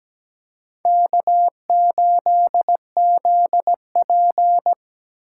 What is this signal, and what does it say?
Morse code, heard as a keyed tone, about 20 wpm, 700 Hz. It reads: K8ZP
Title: Speed Racing - USA 1x2 Call Signs 17wpm